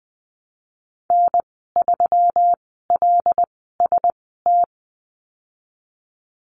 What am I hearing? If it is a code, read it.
Morse code, heard as a keyed tone, about 20 wpm, spaced as for 14 wpm, 700 Hz. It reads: N3LST
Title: 147 Lesson 24g - Letter y - Call Signs 20wpm